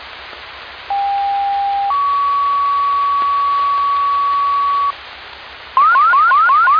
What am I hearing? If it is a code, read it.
Two-tone page, heard as a one-second tone, then a three-second tone, about 800 then 1130 Hz. It is heard from 0.9 to 4.9 s.